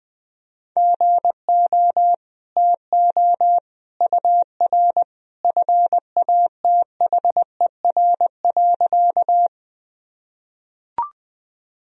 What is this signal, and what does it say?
Morse code, heard as a keyed tone, about 21 wpm, 700 Hz. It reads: GO TO UR FATHER.
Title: Sentences from Top 300 Words and Abbreviations 20wpm